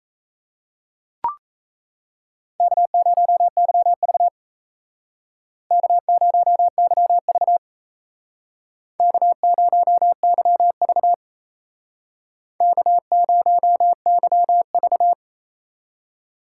Morse code, heard as a keyed tone, about 36 wpm, 700 Hz. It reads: K0YV K0YV K0YV K0YV